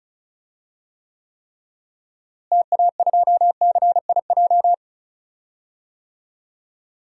Morse code, heard as a keyed tone, about 36 wpm, 700 Hz. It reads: TA2CIJ
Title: DXCC-100 Call Signs - 6 Characters - Repeated 3x - 35wpm